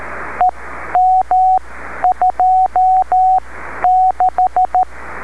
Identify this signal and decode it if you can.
Morse code, heard as a keyed tone, about 13 wpm, spaced as for 10 wpm, 740 Hz, with radio noise. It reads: EM26
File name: beacons / w5gpm-b